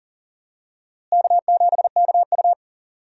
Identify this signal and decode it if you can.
Morse code, heard as a keyed tone, about 40 wpm, 700 Hz. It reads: K7KU
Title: USA Call Signs - 4 Characters - Repeated 2x - 40wpm